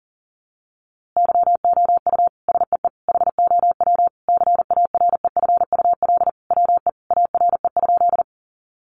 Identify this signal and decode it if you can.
Morse code, heard as a keyed tone, about 40 wpm, 700 Hz. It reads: YOU SEE HOW CAREFUL WE ARE?
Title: A Sound Of Thunder 40wpm